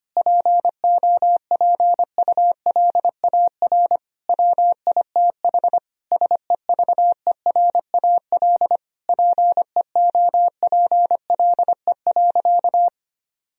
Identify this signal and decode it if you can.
Morse code, heard as a keyed tone, about 25 wpm, 700 Hz. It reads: POPULAR WITH SEVERAL PEOPLE.
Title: Sentences from Top 1000 Words - Review Part 3 - 25wpm